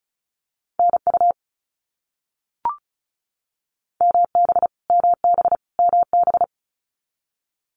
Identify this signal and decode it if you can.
Morse code, heard as a keyed tone, about 34 wpm, 700 Hz. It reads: NU MB MB MB
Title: Canadian Provinces and Territories - Repeated 3x - 35wpm